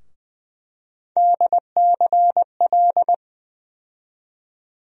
Morse code, heard as a keyed tone, about 20 wpm, 700 Hz. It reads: DCL
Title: USA Call Sign Postfixes - 3 Letters 20wpm